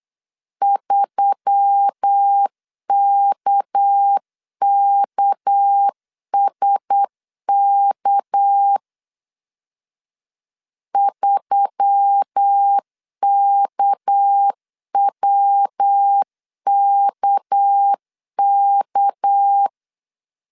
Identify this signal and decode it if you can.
Morse code, heard as a keyed tone, about 8 wpm, 800 Hz. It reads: 3KKSK 3KWKK
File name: LEZIONI / 6 PKX0 / 6 PK 8wpm 3003